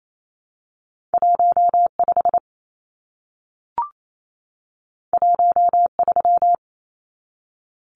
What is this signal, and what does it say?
Morse code, heard as a keyed tone, about 28 wpm, 700 Hz. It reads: 15 13
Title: Numbers - 2-Digit 28wpm